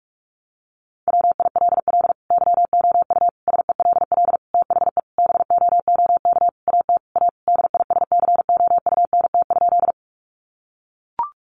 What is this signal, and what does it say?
Morse code, heard as a keyed tone, about 44 wpm, 700 Hz. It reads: WILL YOU SELL THE BOOK AT A DISCOUNT?
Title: Sentences from Top 1200 Words 45wpm